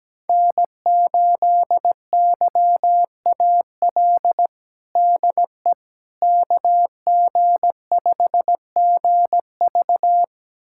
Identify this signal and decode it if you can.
Morse code, heard as a keyed tone, about 17 wpm, 700 Hz. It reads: N8YAL DE KG5GV
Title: USA Callsign DE Callsign - 5 Characters - 17wpm